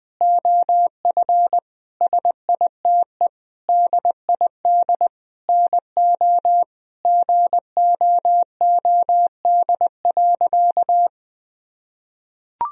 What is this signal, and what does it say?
Morse code, heard as a keyed tone, about 20 wpm, 700 Hz. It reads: OF SITE DID NO GOOD.
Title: Sentences from Top 1000 Words - Review Part 4 - 20wpm